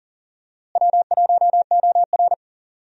Morse code, heard as a keyed tone, about 40 wpm, 700 Hz. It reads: W1OR
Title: USA Call Signs - 4 Characters - Repeated 3x - 40wpm